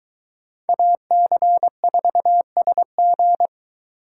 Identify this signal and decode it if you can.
Morse code, heard as a keyed tone, about 22 wpm, 700 Hz. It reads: AC4SG